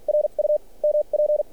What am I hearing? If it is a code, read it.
Morse code, heard as a keyed tone, about 46 wpm, spaced as for 32 wpm, 600 Hz. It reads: RUMP
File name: archived / CRSnet